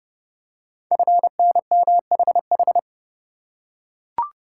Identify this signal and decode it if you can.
Morse code, heard as a keyed tone, about 30 wpm, 700 Hz. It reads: FNMHH
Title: Code Groups - 5 Letters - 30wpm